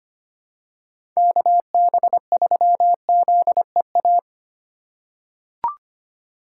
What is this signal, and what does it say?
Morse code, heard as a keyed tone, about 25 wpm, 700 Hz. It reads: KB3ZEA
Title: USA Call Signs - 2x Character Spacing - 25wpm